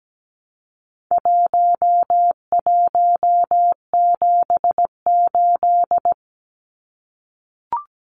Morse code, heard as a keyed tone, about 17 wpm, 700 Hz. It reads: 1178
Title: Numbers - 4-Digit 17wpm